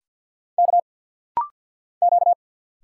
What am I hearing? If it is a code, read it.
Morse code, heard as a keyed tone, about 51 wpm, 700 Hz. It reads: K Q
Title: Single Letter-Number Alt - Rapid-Fire 50wpm